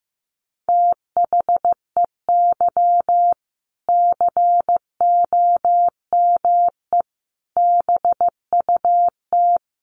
Morse code, heard as a keyed tone, about 15 wpm, 700 Hz. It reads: THEY COME BUT
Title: Sentences from Top 1400 Words 15wpm